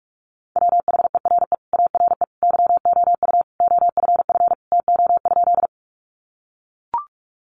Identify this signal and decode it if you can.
Morse code, heard as a keyed tone, about 44 wpm, 700 Hz. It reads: WHERE ARE YOU OFF TO?
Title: Sentences from Top 200 Words 45wpm